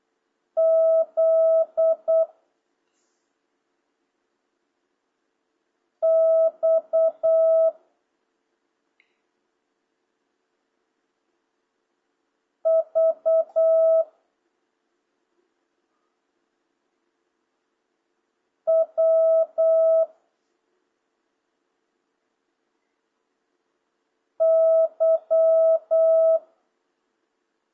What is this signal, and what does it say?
Morse code, heard as a keyed tone, about 8 wpm, 650 Hz. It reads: Z X V W Y